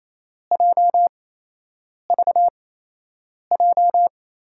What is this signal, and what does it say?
Morse code, heard as a keyed tone, about 28 wpm, spaced as for 8 wpm, 700 Hz. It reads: JVJ
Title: USA Call Signs - 8x Character Spacing - 28wpm